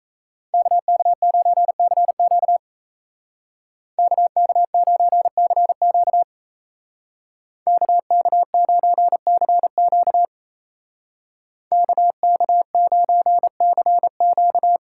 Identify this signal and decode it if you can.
Morse code, heard as a keyed tone, about 37 wpm, 700 Hz. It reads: KK9CQ KK9CQ KK9CQ KK9CQ